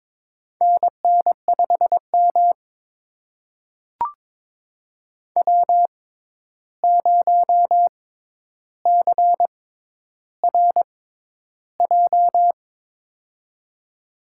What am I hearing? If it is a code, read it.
Morse code, heard as a keyed tone, about 22 wpm, spaced as for 11 wpm, 700 Hz. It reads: NN5M W0CRJ
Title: USA Call Signs - 6x Character Spacing - 22wpm